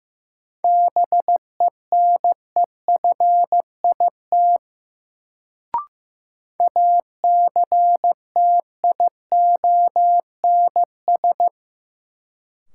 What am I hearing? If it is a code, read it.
Morse code, heard as a keyed tone, about 15 wpm, 700 Hz. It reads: BENEFIT ACTIONS
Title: Sentences from Top 1000 Words - Review Part 6 - New Word Drill - 15wpm